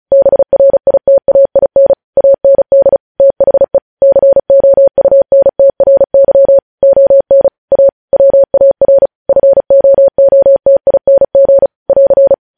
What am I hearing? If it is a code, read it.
Morse code, heard as a keyed tone, about 36 wpm, 550 Hz. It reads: BRITAIN AND THE COUNTRY ON A WAR FOOTING <AR>
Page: news headlines in morse code at 35 wpm